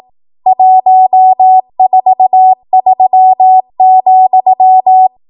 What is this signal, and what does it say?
Morse code, heard as a keyed tone, about 18 wpm, 750 Hz. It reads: 143,
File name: Archive / 18 WPM / 18WPM